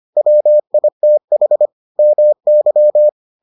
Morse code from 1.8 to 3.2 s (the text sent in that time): MY